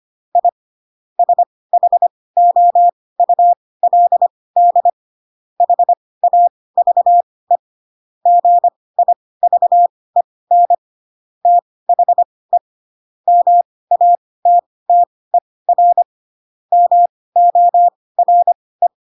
Morse code, heard as a keyed tone, about 25 wpm, spaced as for 18 wpm, 700 Hz. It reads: I SHOULD HAVE GIVEN THE MATTER MORE
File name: Pinocchio ch3 12 25 ik1hge